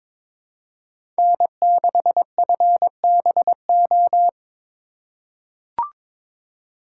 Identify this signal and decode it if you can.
Morse code, heard as a keyed tone, about 22 wpm, 700 Hz. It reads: N6FBO
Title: USA Call Signs - 4x Character Spacing - 22wpm